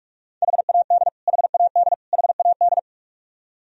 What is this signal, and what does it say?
Morse code, heard as a keyed tone, about 46 wpm, 700 Hz. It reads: HAD HAD HAD
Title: Top 100 Words Repeated 45wpm